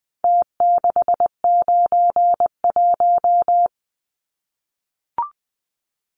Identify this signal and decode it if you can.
Morse code, heard as a keyed tone, about 20 wpm, 700 Hz. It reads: T691
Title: Numbers - Contest Style - 4-Digit 20wpm